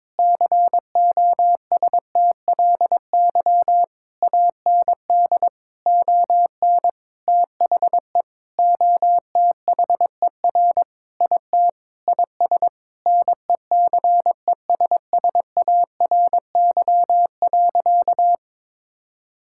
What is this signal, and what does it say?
Morse code, heard as a keyed tone, about 22 wpm, 700 Hz. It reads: COSTLY AND ON THE OTHER IT IS NECESSARY.
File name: practice / 22wpm / Sentences with 8-Word Phrases - 22wpm